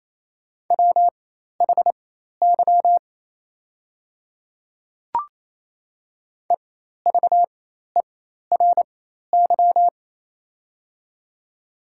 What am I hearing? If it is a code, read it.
Morse code, heard as a keyed tone, about 28 wpm, spaced as for 13 wpm, 700 Hz. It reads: WHY EVERY
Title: 4x Character Spacing - Top 200 Words - 28wpm